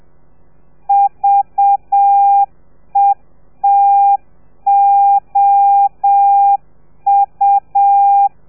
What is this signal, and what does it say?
Morse code, heard as a keyed tone, about 7 wpm, 800 Hz. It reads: VETOU
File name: telegrafia / i2tfj / Lez4 30